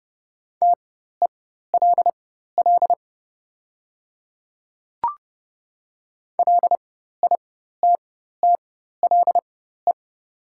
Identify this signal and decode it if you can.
Morse code, heard as a keyed tone, about 30 wpm, spaced as for 14 wpm, 690 Hz. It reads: TELL LITTLE
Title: 4x Character Spacing - Top 300 Words - 30wpm